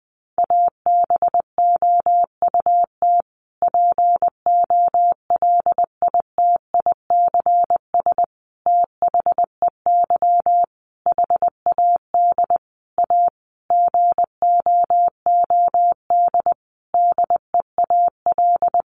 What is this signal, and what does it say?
Morse code, heard as a keyed tone, about 20 wpm, 700 Hz. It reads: ABOUT POLITICS THEY HAD A GOOD DEAL
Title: Sentences from Top 1000 Words 20wpm